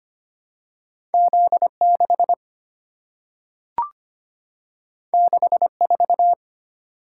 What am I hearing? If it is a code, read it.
Morse code, heard as a keyed tone, about 25 wpm, 700 Hz. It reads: Z6 64
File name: Two Letter-Number Combinations - 25wpm